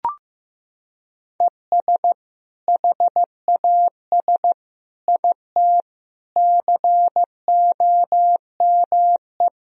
Morse code, heard as a keyed tone, about 15 wpm, 700 Hz. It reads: ES HAS IT COME
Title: Sentences from Top 100 Words and Abbreviations 15wpm